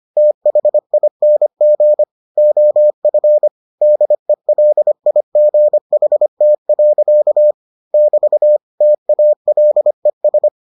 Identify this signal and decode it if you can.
Morse code, heard as a keyed tone, about 25 wpm, 600 Hz. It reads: THING OF DELIGHT. <BT> TALES